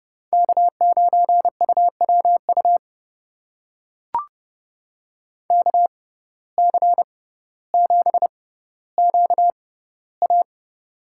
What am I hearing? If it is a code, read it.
Morse code, heard as a keyed tone, about 32 wpm, spaced as for 16 wpm, 700 Hz. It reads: K9UWU KC7QA
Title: USA Call Signs - 5 Characters - 6x Character Spacing - 30wpm